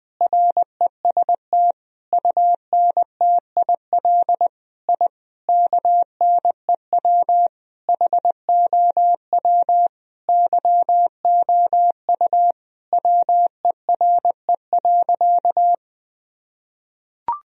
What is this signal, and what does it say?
Morse code, heard as a keyed tone, about 20 wpm, 700 Hz. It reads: REST UNTIL I KNEW HOW YOU WERE.
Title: Sentences from Top 1000 Words - Review Part 1 - 20wpm